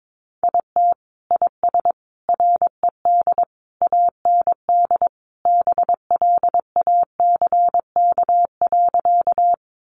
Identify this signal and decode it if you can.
Morse code, heard as a keyed tone, about 22 wpm, 700 Hz. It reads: IT IS RED AND BLACK.
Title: Additional Easy Phrases 22wpm